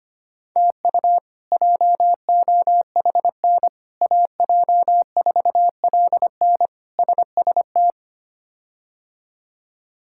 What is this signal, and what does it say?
Morse code, heard as a keyed tone, about 25 wpm, 700 Hz. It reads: TU JOHN AJ4LN SST
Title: K1USN SST QSO Practice - 25wpm